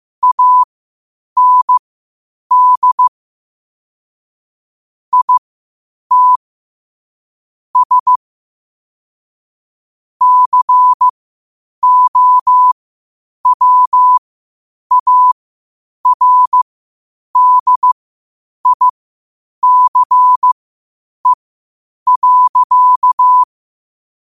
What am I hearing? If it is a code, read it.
Morse code, heard as a keyed tone, about 15 wpm, spaced as for 8 wpm, 1000 Hz. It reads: AND IT S COWARDICE.
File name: qotdfiles / 8wpm